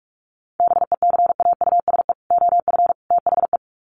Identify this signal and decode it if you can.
Morse code, heard as a keyed tone, about 44 wpm, 700 Hz. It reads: BECAUSE OF THE
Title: Sets of 3 Words 45wpm